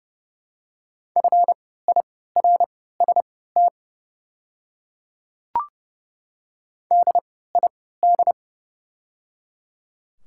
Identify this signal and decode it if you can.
Morse code, heard as a keyed tone, about 30 wpm, spaced as for 17 wpm, 700 Hz. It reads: FIRST DID
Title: 3x Character Spacing - Top 100 Words - 30wpm